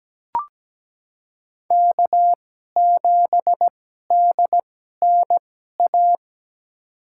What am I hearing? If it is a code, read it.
Morse code, heard as a keyed tone, about 17 wpm, spaced as for 12 wpm, 700 Hz. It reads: K7DNA
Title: USA Call Signs - 5 Characters - 2x Character Spacing - 17wpm